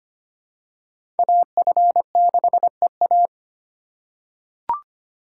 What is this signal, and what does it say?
Morse code, heard as a keyed tone, about 25 wpm, 700 Hz. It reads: AF6EA
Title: USA Call Signs - 5 Characters - 4x Character Spacing - 25wpm